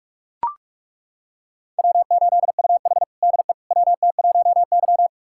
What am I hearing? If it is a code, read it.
Morse code, heard as a keyed tone, about 41 wpm, 700 Hz. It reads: W8UH DE WT1Y
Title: USA Callsign DE Callsign - 4 Characters - 45wpm